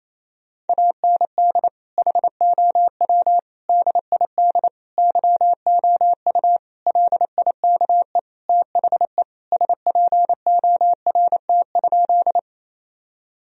Morse code, heard as a keyed tone, about 28 wpm, 700 Hz. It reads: AND HOW DID YOU LIKE THE SPORT?
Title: Sentences from Top 1000 Words - Review Part 4 - 28wpm